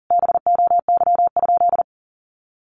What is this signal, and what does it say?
Morse code, heard as a keyed tone, about 40 wpm, 700 Hz. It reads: BOY?
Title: Sentences from Top 1100 Words 40wpm